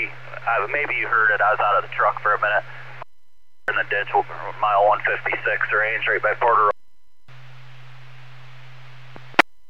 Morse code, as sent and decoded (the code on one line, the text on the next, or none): none